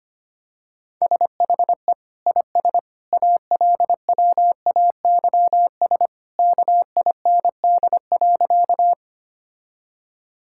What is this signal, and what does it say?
Morse code, heard as a keyed tone, about 25 wpm, 700 Hz. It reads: SHE IS ALWAYS KIND.
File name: practice / 25wpm / Sentences with 2-Word Phrases - 25wpm